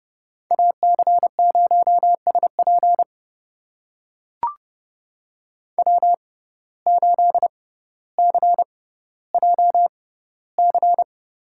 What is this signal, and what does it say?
Morse code, heard as a keyed tone, about 30 wpm, spaced as for 15 wpm, 700 Hz. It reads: AC0SP W8CJC
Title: USA Call Signs - 5 Characters - 6x Character Spacing - 30wpm